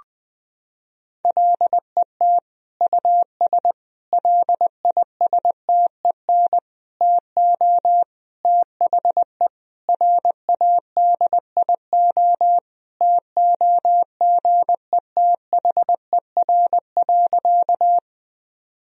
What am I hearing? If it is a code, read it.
Morse code, heard as a keyed tone, about 20 wpm, 700 Hz. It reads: LET US LISTEN TO THE RADIO TOGETHER.
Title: Sentences from Top 1000 Words - Review Part 3 - 20wpm